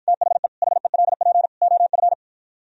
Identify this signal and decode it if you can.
Morse code, heard as a keyed tone, about 52 wpm, 700 Hz. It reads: THE HELP OF